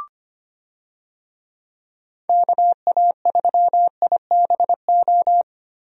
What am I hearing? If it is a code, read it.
Morse code, heard as a keyed tone, about 25 wpm, 700 Hz. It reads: KA3IBO
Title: USA Call Signs - 6 Characters - Repeated 2x - 25wpm